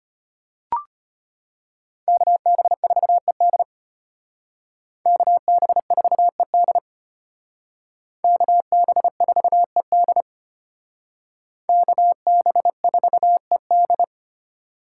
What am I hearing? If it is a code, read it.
Morse code, heard as a keyed tone, about 32 wpm, 700 Hz. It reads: KB4ED KB4ED KB4ED KB4ED